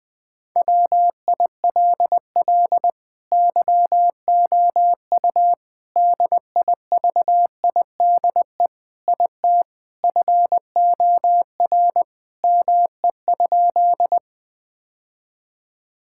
Morse code, heard as a keyed tone, about 21 wpm, 700 Hz. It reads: WILL YOU DIVIDE IT FOR ME?
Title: Sentences from Top 1600 Words 20wpm